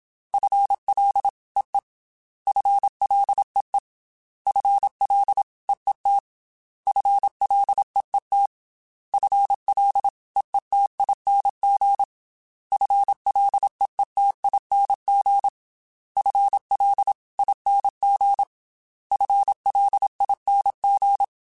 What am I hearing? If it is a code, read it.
Morse code, heard as a keyed tone, about 27 wpm, 800 Hz. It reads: FLEE FLEE FLEET FLEET FLEETING FLEETING FLING FLING
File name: FL / FL mp3s / FL Prefixes 3x2 sp @25